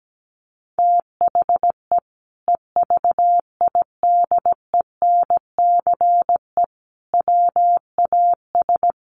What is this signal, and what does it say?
Morse code, heard as a keyed tone, about 17 wpm, 700 Hz. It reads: THE EVIDENCE WAS
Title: Sentences from Top 1300 Words 17wpm